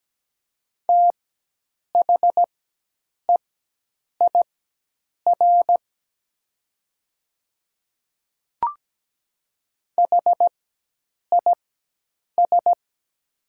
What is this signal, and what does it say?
Morse code, heard as a keyed tone, about 17 wpm, spaced as for 8 wpm, 700 Hz. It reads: THEIR HIS
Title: 4x Character Spacing - Top 100 Words - 17wpm